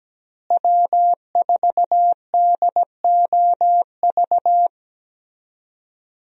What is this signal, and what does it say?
Morse code, heard as a keyed tone, about 17 wpm, 700 Hz. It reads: W4DOV
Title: USA Call Signs - 5 Characters - Repeated 3x - 17wpm